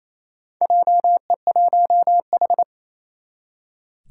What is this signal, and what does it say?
Morse code, heard as a keyed tone, about 28 wpm, 700 Hz. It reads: JE1H
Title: Four Challenging Letter-Number Combinations 28wpm